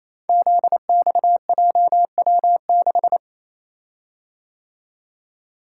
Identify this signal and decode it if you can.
Morse code, heard as a keyed tone, about 28 wpm, 700 Hz. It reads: ZXJW6
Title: Code Groups - 5 Letter:Numbers - 28wpm